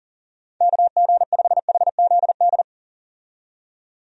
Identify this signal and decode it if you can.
Morse code, heard as a keyed tone, about 40 wpm, 700 Hz. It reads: KG5HZD
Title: USA Call Signs - 6 Characters - 40wpm